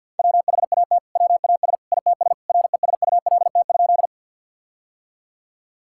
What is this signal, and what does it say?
Morse code, heard as a keyed tone, about 48 wpm, 700 Hz. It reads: WHAT WAS ITS RESULT?
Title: Sentences from Top 1000 Words - Review Part 5 - 50wpm